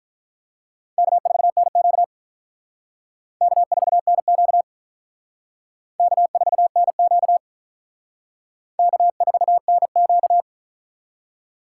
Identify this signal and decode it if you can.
Morse code, heard as a keyed tone, about 45 wpm, 700 Hz. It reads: K4NQ K4NQ K4NQ K4NQ